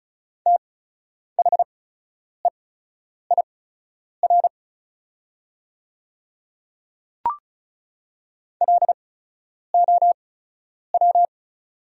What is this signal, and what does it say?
Morse code, heard as a keyed tone, about 34 wpm, spaced as for 10 wpm, 690 Hz. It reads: THEIR LOW
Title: 8x Character Spacing - Top 200 Words - 35wpm